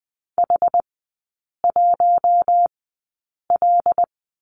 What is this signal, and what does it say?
Morse code, heard as a keyed tone, about 20 wpm, spaced as for 8 wpm, 700 Hz. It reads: H1L
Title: 087 Lesson 15e - Number 5 - Three Characters 20wpm